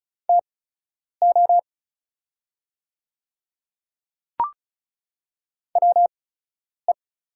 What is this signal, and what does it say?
Morse code, heard as a keyed tone, about 36 wpm, spaced as for 10 wpm, 690 Hz. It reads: TO WE
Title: 8x Character Spacing - Top 100 Words - 35wpm